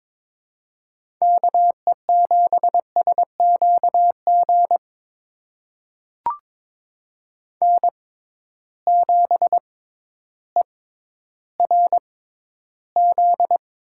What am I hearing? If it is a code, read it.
Morse code, heard as a keyed tone, about 22 wpm, spaced as for 12 wpm, 700 Hz. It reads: KE7SQG N7ERZ
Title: USA Call Signs - 6x Character Spacing - 22wpm